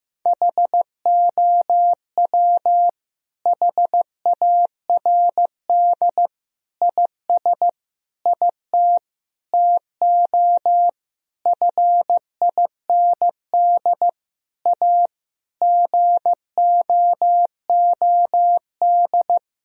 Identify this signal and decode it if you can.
Morse code, heard as a keyed tone, about 15 wpm, 700 Hz. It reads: HOW HARD IS IT TO FIND A GOOD
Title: Sentences from Top 1000 Words - Review Part 1 - 15wpm